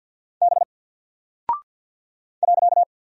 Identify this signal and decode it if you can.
Morse code, heard as a keyed tone, about 49 wpm, 700 Hz. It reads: B .
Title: Single Character (Letter:Number:Punctuation:Prosign) 50wpm